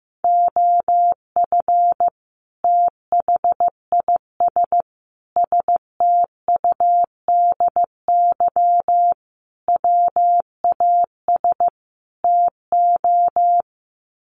Morse code, heard as a keyed tone, about 14 wpm, 700 Hz. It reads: OF THIS STUDY WAS TO
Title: Sets of 5 Words 15wpm